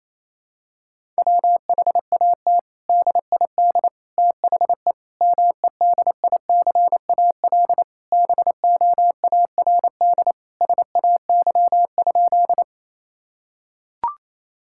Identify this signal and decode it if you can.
Morse code, heard as a keyed tone, about 28 wpm, 700 Hz. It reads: WHAT DID THE MEDICAL BOARD SAY?
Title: Sentences from Top 1000 Words - Review Part 5 - 28wpm